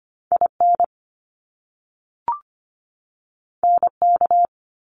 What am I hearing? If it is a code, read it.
Morse code, heard as a keyed tone, about 25 wpm, 700 Hz. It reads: IN NK
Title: Top Two Letter Combinations in Words 25wpm